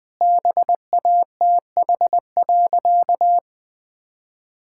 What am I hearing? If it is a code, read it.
Morse code, heard as a keyed tone, about 21 wpm, 700 Hz. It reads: BATH.
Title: Sentences from Top 1600 Words 20wpm